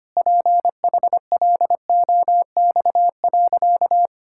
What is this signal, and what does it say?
Morse code, heard as a keyed tone, about 25 wpm, 700 Hz. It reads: PHLOX.
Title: Pangram Challenge v2 - 25wpm